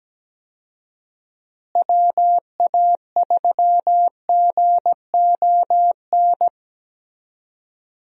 Morse code, heard as a keyed tone, about 17 wpm, 700 Hz. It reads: WA3GON
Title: USA Call Signs - 6 Characters - Repeated 2x - 17wpm